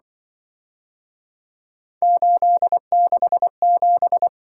Morse code, heard as a keyed tone, about 25 wpm, 700 Hz. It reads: 867